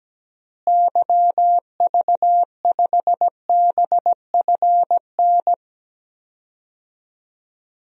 Morse code, heard as a keyed tone, about 17 wpm, 700 Hz. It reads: YV5BFN
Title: DXCC-100 Call Signs - 6 Characters - Repeated 2x - 17wpm